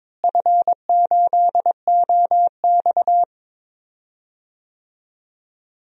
Morse code, heard as a keyed tone, about 22 wpm, 700 Hz. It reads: F8OX
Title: DXCC-100 Call Signs - 4 Characters - Repeated 2x - 22wpm